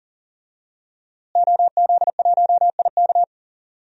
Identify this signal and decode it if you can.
Morse code, heard as a keyed tone, about 40 wpm, 700 Hz. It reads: OZ1IK
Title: DXCC-100 Call Signs - 5 Characters - Repeated 2x - 40wpm